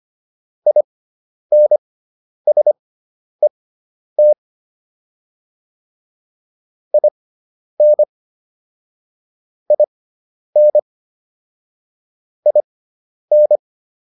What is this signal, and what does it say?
Morse code, heard as a keyed tone, about 25 wpm, spaced as for 10 wpm, 600 Hz. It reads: INSET IN IN IN